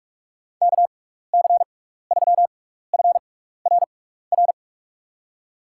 Morse code, heard as a keyed tone, about 44 wpm, spaced as for 15 wpm, 700 Hz. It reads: KC3FRR
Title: USA Call Signs - 6 Characters - 6x Character Spacing - 45wpm